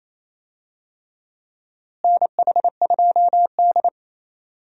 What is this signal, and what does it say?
Morse code, heard as a keyed tone, about 28 wpm, 700 Hz. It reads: NH2D
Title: USA Call Signs - 4 Characters - Repeated 3x - 28wpm